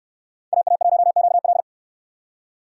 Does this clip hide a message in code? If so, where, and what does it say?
Yes: Morse code from 0.5 to 1.6 s, heard as a keyed tone, about 68 wpm, 700 Hz. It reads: AA1QL